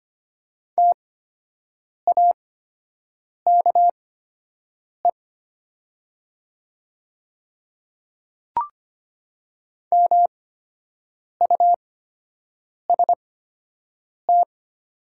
Morse code, heard as a keyed tone, about 25 wpm, spaced as for 7 wpm, 700 Hz. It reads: TAKE MUST
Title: 8x Character Spacing - Top 200 Words - 25wpm